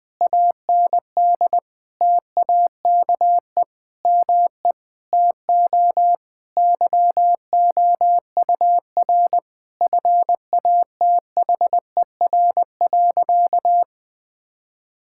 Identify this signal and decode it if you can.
Morse code, heard as a keyed tone, about 20 wpm, 700 Hz. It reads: AND TAKE ME TO YOUR FATHER.
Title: Sentences from Top 1000 Words - Review Part 6 - 20wpm